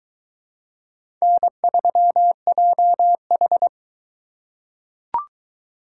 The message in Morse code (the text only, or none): N3JH